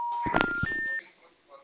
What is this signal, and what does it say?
Special information tone, with speech.